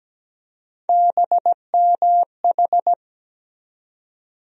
Morse code, heard as a keyed tone, about 17 wpm, 700 Hz. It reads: BMH
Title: USA Call Sign Postfixes - 3 Letters 17wpm